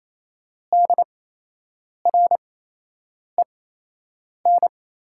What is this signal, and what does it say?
Morse code, heard as a keyed tone, about 28 wpm, spaced as for 8 wpm, 700 Hz. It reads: DREN
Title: 8x Character Spacing - Top 300 Words - 28wpm